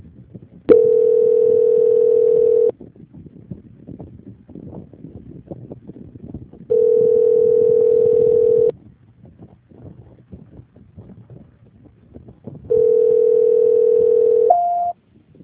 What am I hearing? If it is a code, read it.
Ringback tone (North America), 440 + 480 Hz.